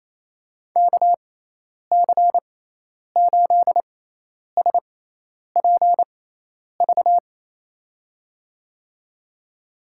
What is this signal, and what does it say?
Morse code, heard as a keyed tone, about 28 wpm, spaced as for 10 wpm, 700 Hz. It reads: KC8SPV
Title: USA Call Signs - 6 Characters - 6x Character Spacing - 28wpm